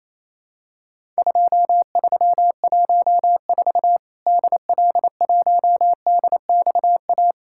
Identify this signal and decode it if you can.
Morse code, heard as a keyed tone, about 28 wpm, 700 Hz. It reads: 2314 DL1DXA
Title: Worked All Europe DX Contest - QTC Exchange - 28wpm